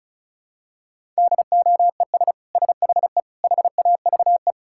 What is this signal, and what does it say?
Morse code, heard as a keyed tone, about 36 wpm, 700 Hz. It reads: DOES SHE HAVE
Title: Sentences from Top 1000 Words - Review Part 6 - 35wpm